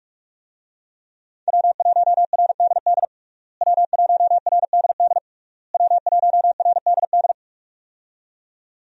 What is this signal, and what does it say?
Morse code, heard as a keyed tone, about 41 wpm, 700 Hz. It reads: W1RDD W1RDD W1RDD
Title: USA Call Signs - 5 Characters - Repeated 3x - 45wpm